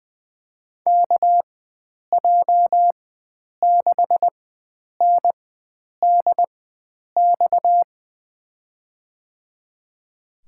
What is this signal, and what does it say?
Morse code, heard as a keyed tone, about 21 wpm, spaced as for 9 wpm, 700 Hz. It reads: KJ6NDX
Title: USA Call Signs - 6 Characters - 4x Character Spacing - 20wpm